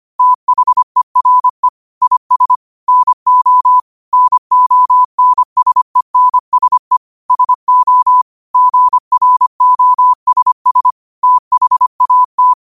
Morse code, heard as a keyed tone, about 25 wpm, 1000 Hz. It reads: THERE IS NO NONSENSE SO GROSS THAT
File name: qotdfiles / 25wpm